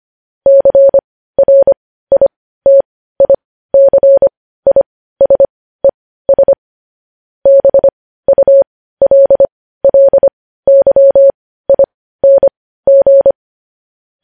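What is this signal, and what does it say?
Morse code, heard as a keyed tone, about 25 wpm, spaced as for 15 wpm, 550 Hz. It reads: CRITICISES BULLYING
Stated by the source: News headlines in Morse code at 15 words per minute, characters sent at 25 WPM.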